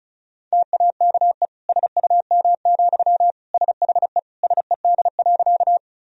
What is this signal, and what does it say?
Morse code, heard as a keyed tone, about 34 wpm, 700 Hz. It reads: TAKE SUM, SHE SED.
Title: Sentences from Top 200 Words and Abbreviations 35wpm